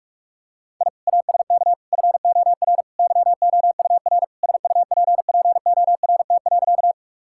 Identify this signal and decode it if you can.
Morse code, heard as a keyed tone, about 46 wpm, 700 Hz. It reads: I ASK FOR YOUR SUPPORT.
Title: Sentences from Top 1000 Words - Review Part 5 - 45wpm